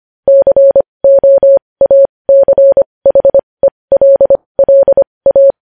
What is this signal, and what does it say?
Morse code, heard as a keyed tone, about 25 wpm, spaced as for 20 wpm, 550 Hz. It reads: COACHELLA
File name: podcast / News headlines-1x-25x20